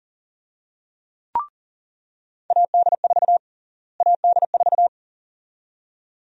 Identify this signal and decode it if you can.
Morse code, heard as a keyed tone, about 39 wpm, 700 Hz. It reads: AD4 AD4
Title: USA Call Sign Prefixes - Repeated 2x - 40wpm